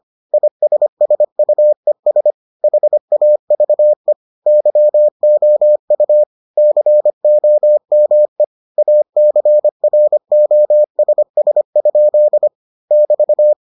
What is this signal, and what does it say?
Morse code, heard as a keyed tone, about 25 wpm, 600 Hz. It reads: ISSUES HAVE YOU COME ACROSS? <BT>